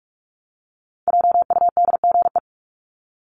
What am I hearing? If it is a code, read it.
Morse code, heard as a keyed tone, about 44 wpm, 700 Hz. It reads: JUDGE
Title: Top 1751-2000 Words 45wpm